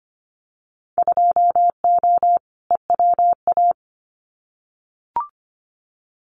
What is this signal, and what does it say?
Morse code, heard as a keyed tone, about 25 wpm, 700 Hz. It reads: 2O EWA
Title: Winter Field Day Exchanges Encore 25wpm